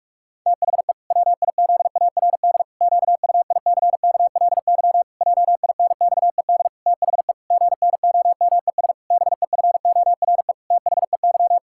Morse code, heard as a keyed tone, about 46 wpm, 700 Hz. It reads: THE WIZARD QUICKLY JINXED THE GNOMES BEFORE THEY
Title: Pangram Challenge v2 - 45wpm